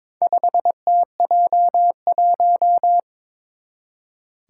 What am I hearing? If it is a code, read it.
Morse code, heard as a keyed tone, about 22 wpm, 700 Hz. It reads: 5TJ1
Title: Four Challenging Letter-Number Combinations 22wpm